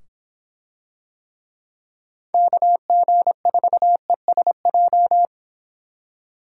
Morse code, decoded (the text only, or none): KG4ESJ